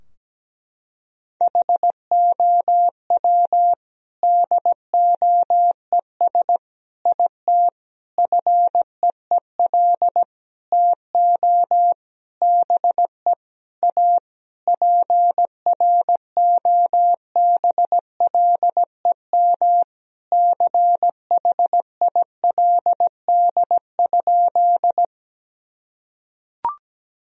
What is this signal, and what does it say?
Morse code, heard as a keyed tone, about 17 wpm, 700 Hz. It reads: HOW DOES IT FEEL TO BE A PROBLEM CHILD?
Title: Sentences from Top 1000 Words - Review Part 2 - 17wpm